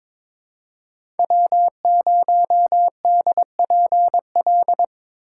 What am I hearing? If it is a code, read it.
Morse code, heard as a keyed tone, about 22 wpm, 700 Hz. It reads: W0DPL
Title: USA Call Signs - 5 Characters - 2x Character Spacing - 22wpm